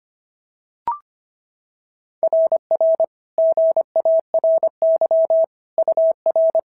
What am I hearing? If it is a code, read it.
Morse code, heard as a keyed tone, about 25 wpm, 650 Hz. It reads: RR GARY UR
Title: SOTA QSO Practice 25wpm